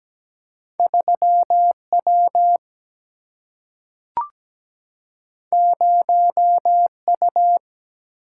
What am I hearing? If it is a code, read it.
Morse code, heard as a keyed tone, about 17 wpm, 700 Hz. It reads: 3W 0U
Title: Single-Letter Single-Number 17wpm